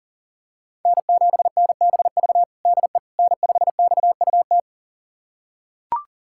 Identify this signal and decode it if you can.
Morse code, heard as a keyed tone, about 40 wpm, 700 Hz. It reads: N7NBV DE N5XUT
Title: USA Callsign DE Callsign - 5 Characters - 40wpm